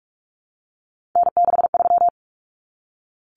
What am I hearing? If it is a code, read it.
Morse code, heard as a keyed tone, about 44 wpm, 700 Hz. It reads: N63
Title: Common Cut Numbers - 3-Digit 45wpm